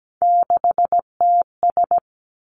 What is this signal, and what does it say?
Morse code, heard as a keyed tone, about 17 wpm, 700 Hz. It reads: 6TS
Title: Three Challenging Letter-Number Combinations 17wpm